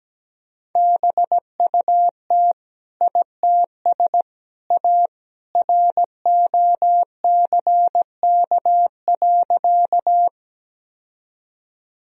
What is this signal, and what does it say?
Morse code, heard as a keyed tone, about 17 wpm, 700 Hz. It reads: BUT ITS A ROCK.
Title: Sentences from Top 1000 Words - Review Part 2 - 17wpm